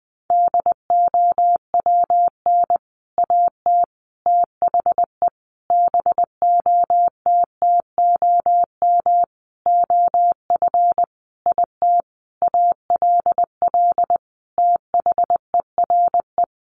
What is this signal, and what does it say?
Morse code, heard as a keyed tone, about 20 wpm, 700 Hz. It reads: DOWN AT THE BOTTOM OF IT ALL THERE
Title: Sentences from Top 1100 Words 20wpm